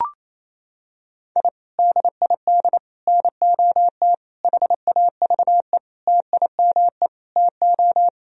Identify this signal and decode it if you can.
Morse code, heard as a keyed tone, about 28 wpm, 700 Hz. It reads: I DID NOT HAVE TIME TO
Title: Sentences from Top 1000 Words - Review Part 3 - 28wpm